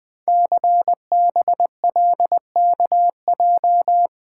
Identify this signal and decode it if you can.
Morse code, heard as a keyed tone, about 20 wpm, 700 Hz. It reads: CBLKJ
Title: Code Groups - 5 Letters - 20wpm